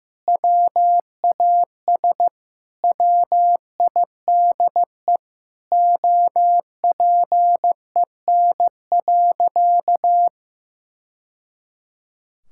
Morse code, heard as a keyed tone, about 15 wpm, 700 Hz. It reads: WAS WIDE OPEN.
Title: Sentences from Top 1000 Words - Review Part 6 - 15wpm